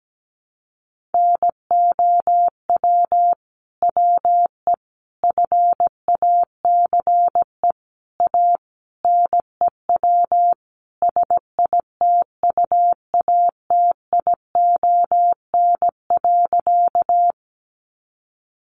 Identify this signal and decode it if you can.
Morse code, heard as a keyed tone, about 17 wpm, 700 Hz. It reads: NOW WE FACE A NEW SITUATION.
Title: Sentences from Top 1200 Words 17wpm